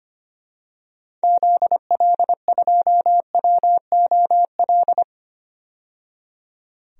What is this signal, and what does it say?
Morse code, heard as a keyed tone, about 25 wpm, 700 Hz. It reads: ZL2WOL
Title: DXCC-100 Call Signs - 6 Characters - 25wpm